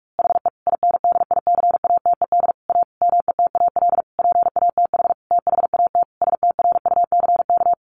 Morse code, heard as a keyed tone, about 44 wpm, 700 Hz. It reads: HE INDICATED A METAL PATH THAT STRUCK
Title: A Sound Of Thunder 45wpm